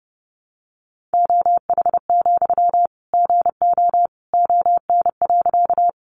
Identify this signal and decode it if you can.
Morse code, heard as a keyed tone, about 30 wpm, 700 Hz. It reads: OH, GO ON.
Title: Sentences from Top 500 Words 30wpm